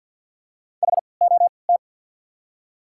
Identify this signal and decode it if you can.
Morse code, heard as a keyed tone, about 48 wpm, spaced as for 28 wpm, 700 Hz. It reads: HOT
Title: 3x Character Spacing - Top 500 Words - 50wpm